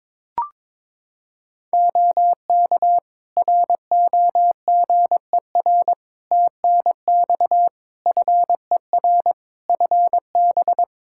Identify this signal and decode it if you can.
Morse code, heard as a keyed tone, about 22 wpm, 700 Hz. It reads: OK ROGER TNX FER FB
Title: Practice QSO - Exchange 4 - Ending - 22wpm